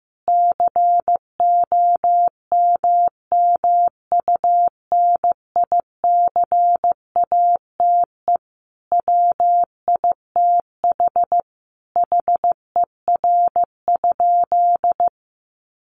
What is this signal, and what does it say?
Morse code, heard as a keyed tone, about 15 wpm, 700 Hz. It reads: COMMUNICATE WITH HER?
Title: Sentences from Top 1300 Words 15wpm